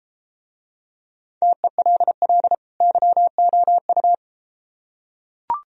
Morse code, heard as a keyed tone, about 33 wpm, 700 Hz. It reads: TELL YOU